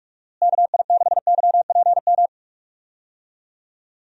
Morse code, heard as a keyed tone, about 46 wpm, 700 Hz. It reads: KI6YPM
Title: USA Call Signs - 6 Characters - Repeated 3x - 45wpm